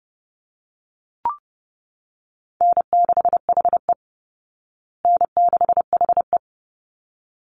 Morse code, heard as a keyed tone, about 30 wpm, 700 Hz. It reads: N6HE N6HE
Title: Easy Call Signs Repeated 2x - 30wpm